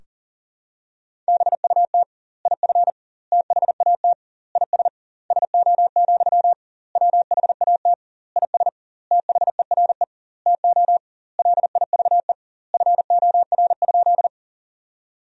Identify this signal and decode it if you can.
Morse code, heard as a keyed tone, about 37 wpm, 700 Hz. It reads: BUT IF THAT IS SO, WHAT IS THERE TO LIVE FOR?
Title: 2x Word-Spacing - Sentences from Top 200 Words - 40wpm